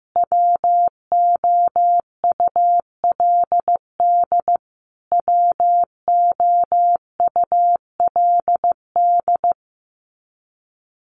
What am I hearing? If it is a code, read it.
Morse code, heard as a keyed tone, about 15 wpm, 700 Hz. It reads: WOULD WOULD
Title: Top 300 Words Repeated 15wpm